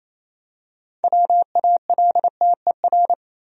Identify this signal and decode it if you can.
Morse code, heard as a keyed tone, about 28 wpm, 700 Hz. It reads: WALTER
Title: USA Common Names - 28wpm